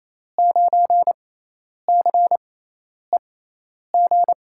Morse code, heard as a keyed tone, about 28 wpm, spaced as for 10 wpm, 700 Hz. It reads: 9CEG
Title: USA Call Signs - 6 Characters - 6x Character Spacing - 28wpm